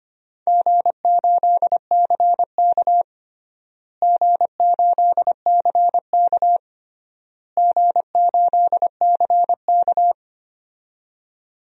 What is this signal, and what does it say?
Morse code, heard as a keyed tone, about 25 wpm, 700 Hz. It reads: G8CK G8CK G8CK
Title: DXCC-100 Call Signs - 4 Characters - Repeated 3x - 25wpm